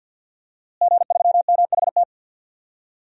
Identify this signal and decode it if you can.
Morse code, heard as a keyed tone, about 46 wpm, 700 Hz. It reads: G3MHT
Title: DXCC-100 Call Signs - 5 Characters - Repeated 3x - 50wpm